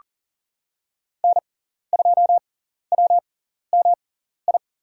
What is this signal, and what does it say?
Morse code, heard as a keyed tone, about 40 wpm, spaced as for 14 wpm, 700 Hz. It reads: N2WMI
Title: USA Call Signs - 5 Characters - 6x Character Spacing - 40wpm